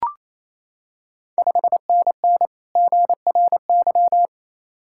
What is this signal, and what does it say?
Morse code, heard as a keyed tone, about 28 wpm, 700 Hz. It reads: 5NN GRY
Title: Ontario QSO Party Exchange 28wpm